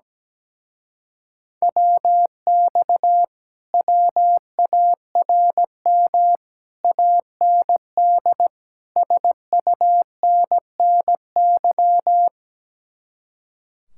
Morse code, heard as a keyed tone, about 17 wpm, 700 Hz. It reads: WX WARM AND SUNNY
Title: Speed-Racing Common QSO Phrases - 17wpm